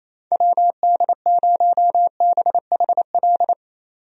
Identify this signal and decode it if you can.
Morse code, heard as a keyed tone, about 28 wpm, 700 Hz. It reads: WD0BHL
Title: USA Call Signs - 6 Characters - 28wpm